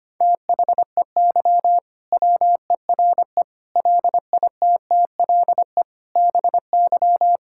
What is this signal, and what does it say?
Morse code, heard as a keyed tone, about 25 wpm, 700 Hz. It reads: THEY WERE LITTLE BY
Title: Sentences from Top 1600 Words 25wpm